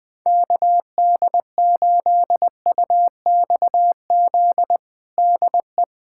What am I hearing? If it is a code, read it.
Morse code, heard as a keyed tone, about 20 wpm, 700 Hz. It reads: KD8UXZ DE
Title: USA Callsign DE Callsign - 6 Characters - 20wpm